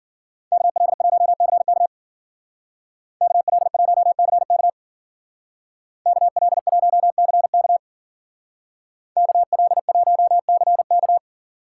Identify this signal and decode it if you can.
Morse code, heard as a keyed tone, about 50 wpm, 700 Hz. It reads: KL1CK KL1CK KL1CK KL1CK
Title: Speed Racing - USA 2x2 Call Signs 40wpm